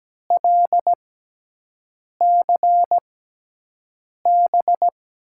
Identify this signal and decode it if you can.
Morse code, heard as a keyed tone, about 17 wpm, spaced as for 6 wpm, 700 Hz. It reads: LCB
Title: USA Call Signs - 6 Characters - 6x Character Spacing - 17wpm